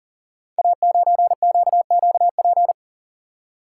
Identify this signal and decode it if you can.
Morse code, heard as a keyed tone, about 39 wpm, 700 Hz. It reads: A9QQP
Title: Code Groups - 5 Letter:Numbers - 40wpm